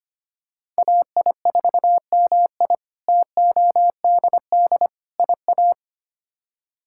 Morse code, heard as a keyed tone, about 25 wpm, 700 Hz. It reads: AI4MI TODD IA
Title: K1USN SST QSO Practice - 25wpm